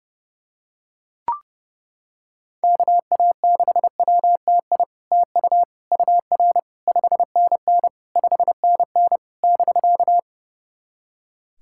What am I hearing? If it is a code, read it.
Morse code, heard as a keyed tone, about 30 wpm, 700 Hz. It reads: KA6WTI TU UR 5NN 5NN <BK>
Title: SOTA QSO Practice 30wpm